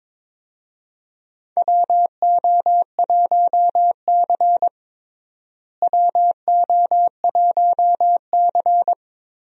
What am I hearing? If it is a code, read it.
Morse code, heard as a keyed tone, about 22 wpm, 700 Hz. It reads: WO1C WO1C
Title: USA Call Signs - 4 Characters - Repeated 3x - 22wpm